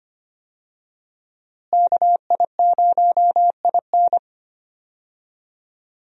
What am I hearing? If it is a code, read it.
Morse code, heard as a keyed tone, about 25 wpm, 700 Hz. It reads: KI0IN
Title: USA Call Signs - 5 Characters - Repeated 2x - 25wpm